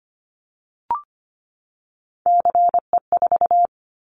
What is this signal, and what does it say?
Morse code, heard as a keyed tone, about 25 wpm, 700 Hz. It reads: CE4
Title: Callsign Prefixes for Top 100 DXCC - 25wpm